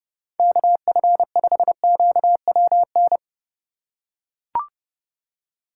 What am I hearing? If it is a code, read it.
Morse code, heard as a keyed tone, about 30 wpm, 700 Hz. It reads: KF5QWN
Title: USA Call Signs - 6 Characters - 4x Character Spacing - 30wpm